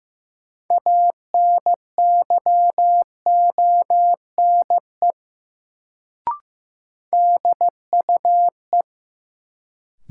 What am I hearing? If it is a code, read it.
Morse code, heard as a keyed tone, about 15 wpm, 700 Hz. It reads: ANYONE DUE
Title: Sets of 5 Words Encore - New Word Drill - 15wpm